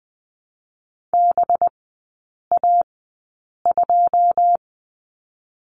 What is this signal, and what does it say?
Morse code, heard as a keyed tone, about 20 wpm, 700 Hz. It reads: B A 2
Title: 198 Lesson 32e - Letter k - Three Characters 20wpm